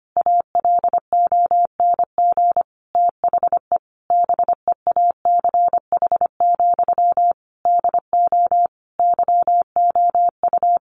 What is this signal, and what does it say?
Morse code, heard as a keyed tone, about 25 wpm, 700 Hz. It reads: ALONG THE BEACH, DO YOU
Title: Sentences from Top 1500 Words 25wpm